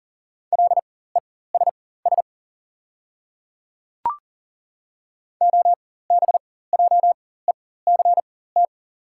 Morse code, heard as a keyed tone, about 40 wpm, spaced as for 19 wpm, 700 Hz. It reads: LESS OBJECT
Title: 4x Character Spacing - Top 500 Words - 40wpm